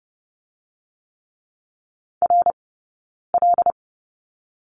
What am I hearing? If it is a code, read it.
Morse code, heard as a keyed tone, about 30 wpm, 700 Hz. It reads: R L
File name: 092 Lesson 16d - Letter c - Two Characters 30wpm